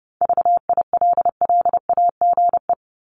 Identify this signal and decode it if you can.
Morse code, heard as a keyed tone, about 30 wpm, 700 Hz. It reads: VILLAGE
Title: Top 1751-2000 Words 30wpm